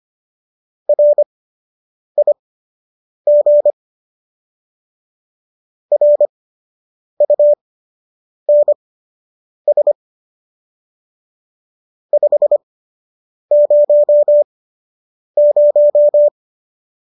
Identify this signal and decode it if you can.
Morse code, heard as a keyed tone, about 25 wpm, spaced as for 8 wpm, 590 Hz. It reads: RIG RUNS 500